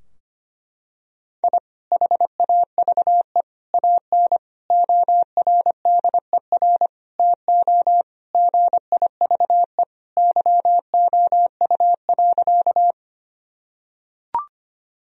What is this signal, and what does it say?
Morse code, heard as a keyed tone, about 25 wpm, 700 Hz. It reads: I HAVE AN ORDER TO GIVE YOU.
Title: Sentences from Top 1000 Words - Review Part 2 - 25wpm